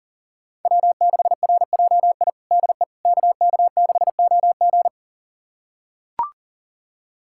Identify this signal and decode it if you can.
Morse code, heard as a keyed tone, about 40 wpm, 700 Hz. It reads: W6RJI DE KK6OG
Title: USA Callsign DE Callsign - 5 Characters - 40wpm